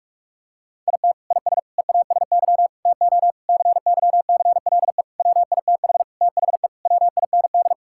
Morse code, heard as a keyed tone, about 46 wpm, 700 Hz. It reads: IT IS EASY TO CYCLE WITH THE WIND
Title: Sentences from Top 1000 Words - Review Part 6 - 45wpm